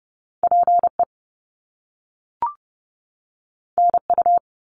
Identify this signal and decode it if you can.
Morse code, heard as a keyed tone, about 30 wpm, 700 Hz. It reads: PE NU
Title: Canadian Provinces and Territories 30wpm